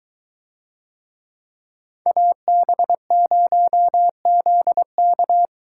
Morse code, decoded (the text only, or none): AB0ZK